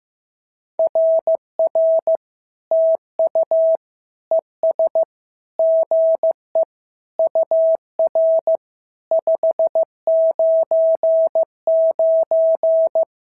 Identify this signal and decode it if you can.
Morse code, heard as a keyed tone, about 14 wpm, 650 Hz. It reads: RR TU ES GE UR 599
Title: SOTA QSO Practice 15wpm